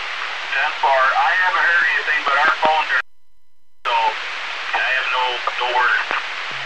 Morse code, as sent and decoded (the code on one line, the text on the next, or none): none